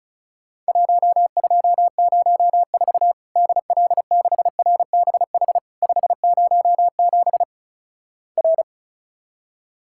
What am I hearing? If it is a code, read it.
Morse code, heard as a keyed tone, about 36 wpm, 700 Hz. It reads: 1204 DL6RBH 507 R
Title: Worked All Europe DX Contest - QTC Exchange - 35wpm